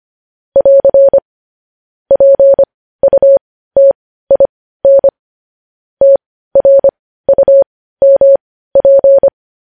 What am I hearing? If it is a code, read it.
Morse code, heard as a keyed tone, about 25 wpm, spaced as for 15 wpm, 550 Hz. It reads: <AR> PUTIN TRUMP
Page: news-headlines-in-morse-code-at-15-wpm